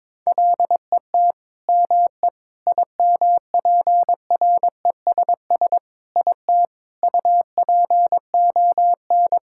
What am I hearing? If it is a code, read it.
Morse code, heard as a keyed tone, about 22 wpm, 700 Hz. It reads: LET ME IMPRESS IT UPON
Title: Sentences from Top 1600 Words 22wpm